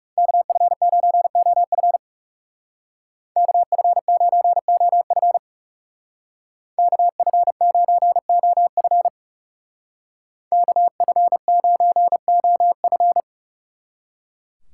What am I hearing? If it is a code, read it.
Morse code, heard as a keyed tone, about 36 wpm, 700 Hz. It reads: KF9OF KF9OF KF9OF KF9OF
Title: Speed Racing - USA 2x2 Call Signs 30wpm